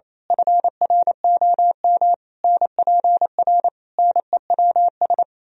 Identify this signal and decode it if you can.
Morse code, heard as a keyed tone, about 27 wpm, 700 Hz. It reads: FROM NPR NEWS
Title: Speed-Racing - Sets of 5 Words 28wpm